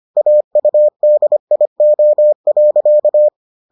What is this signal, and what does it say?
Morse code, heard as a keyed tone, about 25 wpm, 600 Hz. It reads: AUDIO.